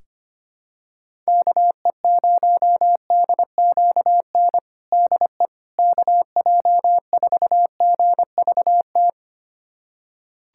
Morse code, heard as a keyed tone, about 25 wpm, 700 Hz. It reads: KE0DQN DE KJ4GVT
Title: USA Callsign DE Callsign - 6 Characters - 25wpm